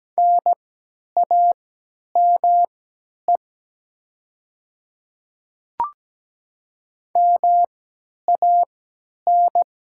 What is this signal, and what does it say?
Morse code, heard as a keyed tone, about 17 wpm, spaced as for 10 wpm, 700 Hz. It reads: NAME MAN
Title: 3x Character Spacing - Top 500 Words - 17wpm